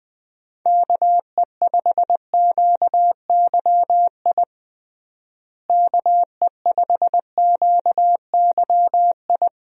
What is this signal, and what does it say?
Morse code, heard as a keyed tone, about 20 wpm, 700 Hz. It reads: KE5QYI KE5QYI
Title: USA Call Signs - 6 Characters - Repeated 2x - 20wpm